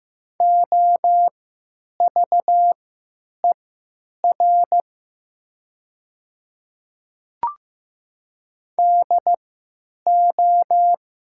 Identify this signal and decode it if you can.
Morse code, heard as a keyed tone, about 15 wpm, spaced as for 8 wpm, 700 Hz. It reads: OVER DO
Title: 3x Character Spacing - Top 100 Words - 15wpm